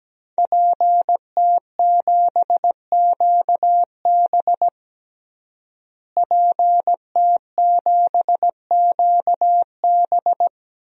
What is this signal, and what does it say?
Morse code, heard as a keyed tone, about 17 wpm, 700 Hz. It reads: PT7QB PT7QB
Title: DXCC-100 Call Signs - 5 Characters - Repeated 3x - 17wpm